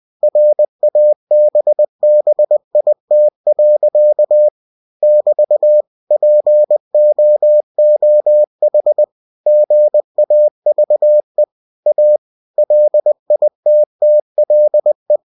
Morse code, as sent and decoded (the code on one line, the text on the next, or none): .-. .- -... -... .. - .-.-.- / -...- / .--. --- --- .... / --. .- ...- . / .- / .-.. .. - - .-.. .
RABBIT. <BT> POOH GAVE A LITTLE